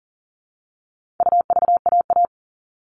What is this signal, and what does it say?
Morse code, heard as a keyed tone, about 40 wpm, 700 Hz. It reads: UVAA
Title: All Cut Numbers - 4-Digit 40wpm